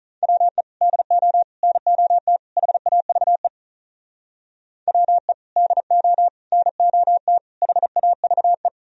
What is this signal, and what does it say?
Morse code, heard as a keyed tone, about 45 wpm, 700 Hz. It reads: WE DO NOT HAVE WE DO NOT HAVE